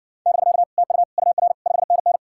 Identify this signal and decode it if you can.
Morse code, heard as a keyed tone, about 60 wpm, 700 Hz. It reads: <BK> TU UR 5NN